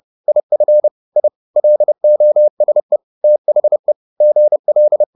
Morse code, heard as a keyed tone, about 30 wpm, 600 Hz. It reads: IF I LOSE THE GL
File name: the old man and the sea / the old man and the sea 1 30wpm